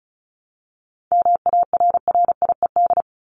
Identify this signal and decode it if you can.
Morse code, heard as a keyed tone, about 34 wpm, 700 Hz. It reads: MARRIED
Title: ED Suffix - Top 100 Words with Base Words 35wpm